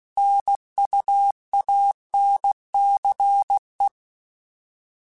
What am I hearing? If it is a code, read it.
Morse code, heard as a keyed tone, about 16 wpm, 800 Hz. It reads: NUANCE
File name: Alphabetical / Alphabetical mp3s / Ns @15